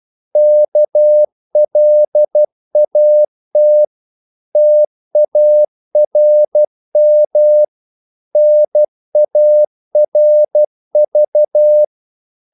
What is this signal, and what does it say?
Morse code, heard as a keyed tone, about 12 wpm, 600 Hz. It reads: KLAT TARM NARV
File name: Korte ord DK 12wpm